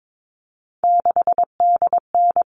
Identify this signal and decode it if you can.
Morse code, heard as a keyed tone, about 22 wpm, 700 Hz. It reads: 6DN
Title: All Cut Numbers - 3-Digit 22wpm